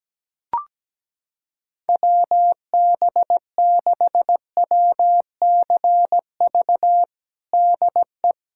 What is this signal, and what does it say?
Morse code, heard as a keyed tone, about 17 wpm, 700 Hz. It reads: WB6WCV DE
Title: USA Callsign DE Callsign - 6 Characters - 17wpm